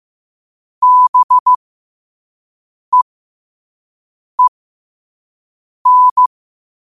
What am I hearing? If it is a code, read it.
Morse code, heard as a keyed tone, about 15 wpm, spaced as for 5 wpm, 1000 Hz. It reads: BEEN
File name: qotdfiles / 5wpm